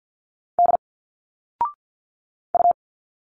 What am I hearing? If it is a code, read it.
Morse code, heard as a keyed tone, about 50 wpm, 700 Hz. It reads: D U
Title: Single Letters 50wpm